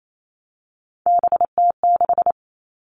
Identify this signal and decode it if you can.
Morse code, heard as a keyed tone, about 28 wpm, 700 Hz. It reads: BT6
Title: All Cut Numbers - 3-Digit 28wpm